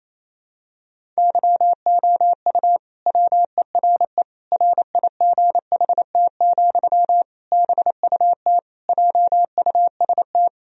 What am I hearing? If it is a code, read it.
Morse code, heard as a keyed tone, about 28 wpm, 700 Hz. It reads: YOU WERE RIGHT, BUT JUST
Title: Sentences from Top 1000 Words - Review Part 2 - 28wpm